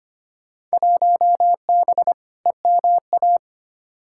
Morse code, heard as a keyed tone, about 25 wpm, 700 Hz. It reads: 1B EMA
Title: Field Day Exchanges - 25wpm